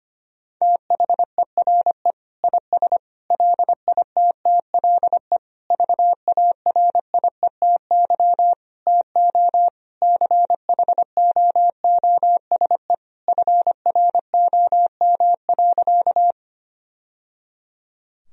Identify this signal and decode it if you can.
Morse code, heard as a keyed tone, about 25 wpm, 700 Hz. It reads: THERE IS LITTLE VARIETY TO CHOOSE FROM.
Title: Sentences from Top 1000 Words - Review Part 4 - 25wpm